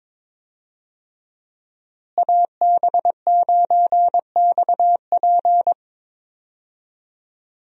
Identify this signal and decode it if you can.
Morse code, heard as a keyed tone, about 22 wpm, 700 Hz. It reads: AB9XP
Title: USA Call Signs - 5 Characters - Repeated 2x - 22wpm